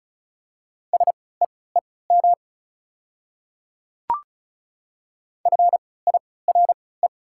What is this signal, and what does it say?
Morse code, heard as a keyed tone, about 36 wpm, spaced as for 20 wpm, 690 Hz. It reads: SEEM FIRE
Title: 3x Character Spacing - Top 500 Words - 35wpm